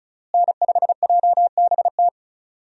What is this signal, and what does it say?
Morse code, heard as a keyed tone, about 35 wpm, 700 Hz. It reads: N5JBT
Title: USA Call Signs - 5 Characters - Repeated 3x - 35wpm